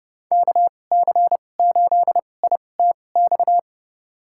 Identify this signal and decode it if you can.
Morse code, heard as a keyed tone, about 30 wpm, spaced as for 22 wpm, 700 Hz. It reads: KC8ITX
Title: USA Call Signs - 6 Characters - 2x Character Spacing - 30wpm